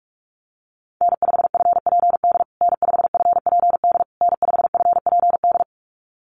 Morse code, heard as a keyed tone, about 44 wpm, 690 Hz. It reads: N5FPD N5FPD N5FPD
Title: USA Call Signs Repeated 3x 45wpm